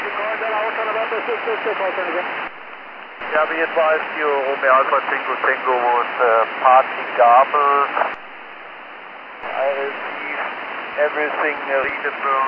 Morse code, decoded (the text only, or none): none